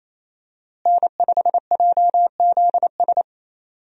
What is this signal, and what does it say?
Morse code, heard as a keyed tone, about 28 wpm, 700 Hz. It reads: N5JZS
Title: USA Call Signs - 5 Characters - Repeated 2x - 28wpm